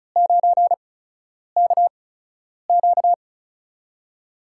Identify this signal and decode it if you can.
Morse code, heard as a keyed tone, about 36 wpm, spaced as for 10 wpm, 700 Hz. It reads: 9KQ
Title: USA Call Signs - 4 Characters - 8x Character Spacing - 35wpm